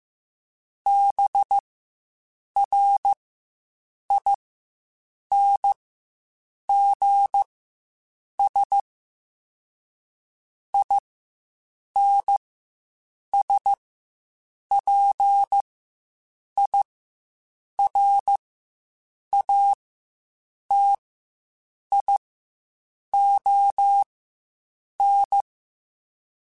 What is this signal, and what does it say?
Morse code, heard as a keyed tone, about 15 wpm, spaced as for 7 wpm, 800 Hz. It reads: BRINGS INSPIRATION
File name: Quotes Spaced / Quotes Spaced-mp3s / Persperation sp @10